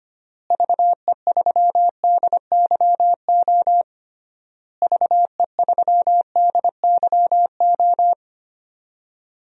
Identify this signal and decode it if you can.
Morse code, heard as a keyed tone, about 25 wpm, 700 Hz. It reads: VE3DYO VE3DYO
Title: DXCC-100 Call Signs - 6 Characters - Repeated 3x - 25wpm